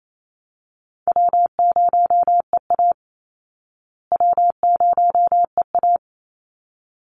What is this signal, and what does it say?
Morse code, heard as a keyed tone, about 28 wpm, 700 Hz. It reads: W0EA W0EA
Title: Call Signs Repeated 3x - 28wpm